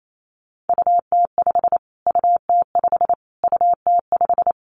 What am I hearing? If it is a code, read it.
Morse code, heard as a keyed tone, about 28 wpm, 700 Hz. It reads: UT5 UT5 UT5
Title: Callsign Prefixes for Top 25 DXCC - Repeated 3x - 28wpm